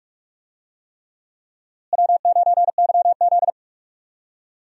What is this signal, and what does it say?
Morse code, heard as a keyed tone, about 46 wpm, 700 Hz. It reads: W9YZ
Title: USA Call Signs - 4 Characters - Repeated 3x - 45wpm